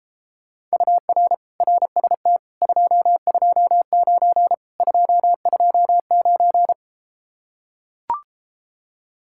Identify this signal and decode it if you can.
Morse code, heard as a keyed tone, about 33 wpm, 700 Hz. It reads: UR RST 229 229